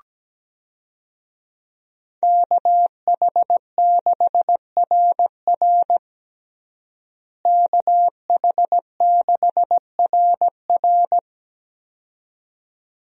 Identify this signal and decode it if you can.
Morse code, heard as a keyed tone, about 17 wpm, 700 Hz. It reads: KH6RR KH6RR
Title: USA Call Signs - 5 Characters - Repeated 2x - 17wpm